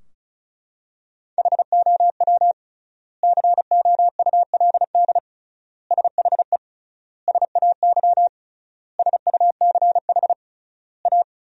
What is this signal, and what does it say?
Morse code, heard as a keyed tone, about 36 wpm, 700 Hz. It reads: HOW COULD SHE SAY SUCH A
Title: 3x Word-Spacing - Sentences from Top 200 Words - 35wpm